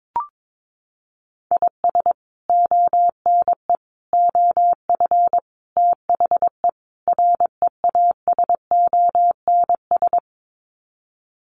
Morse code, heard as a keyed tone, about 22 wpm, 700 Hz. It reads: IS ONE OF THE REASONS
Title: Sets of 5 Words Encore2 22wpm